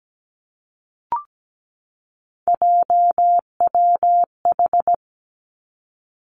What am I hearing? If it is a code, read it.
Morse code, heard as a keyed tone, about 17 wpm, 700 Hz. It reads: JWH
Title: Three Challenging Letter-Number Combinations 17wpm